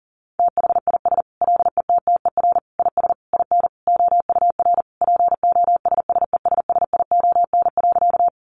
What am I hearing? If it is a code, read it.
Morse code, heard as a keyed tone, about 40 wpm, 700 Hz. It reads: THIS LETTER IS IN OUR POSSESSION.
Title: Sentences from Top 1400 Words 40wpm